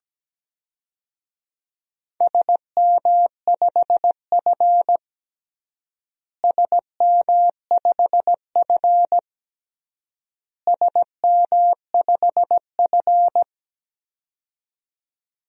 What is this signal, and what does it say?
Morse code, heard as a keyed tone, about 17 wpm, 700 Hz. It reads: SM5F SM5F SM5F
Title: DXCC-100 Call Signs - 4 Characters - Repeated 3x - 17wpm